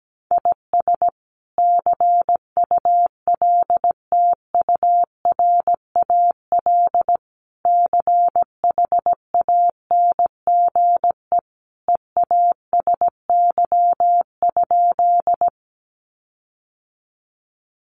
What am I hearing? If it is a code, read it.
Morse code, heard as a keyed tone, about 17 wpm, 700 Hz. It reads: IS CULTURAL CHANGE EASY?
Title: Sentences from Top 1500 Words 17wpm